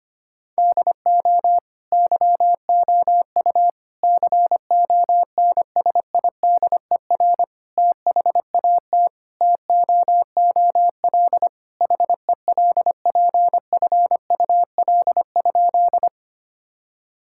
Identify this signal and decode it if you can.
Morse code, heard as a keyed tone, about 25 wpm, 700 Hz. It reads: DO YOU CONSIDER THAT TOOL HELPFUL?
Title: Sentences from Top 1000 Words - Review Part 6 - 25wpm